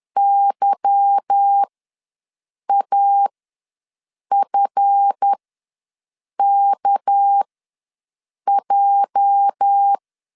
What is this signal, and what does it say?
Morse code, heard as a keyed tone, about 10 wpm, spaced as for 6 wpm, 800 Hz. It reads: YAFKJ
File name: LEZIONI / 10 SIGNS / 10B INT ws 10wpm 4134